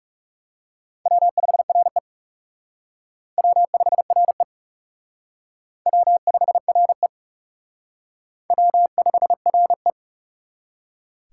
Morse code, heard as a keyed tone, about 38 wpm, 700 Hz. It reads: W5RE W5RE W5RE W5RE
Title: Speed Racing - USA 1x2 Call Signs 30wpm